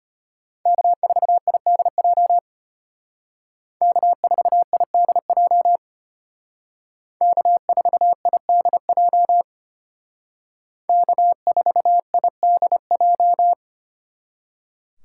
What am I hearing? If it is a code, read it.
Morse code, heard as a keyed tone, about 32 wpm, 700 Hz. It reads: K4IDJ K4IDJ K4IDJ K4IDJ